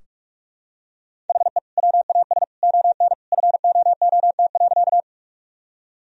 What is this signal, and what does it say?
Morse code, heard as a keyed tone, about 41 wpm, 700 Hz. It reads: HE WAS ON FOOT.
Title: Sentences from Top 1000 Words - Review Part 3 - 45wpm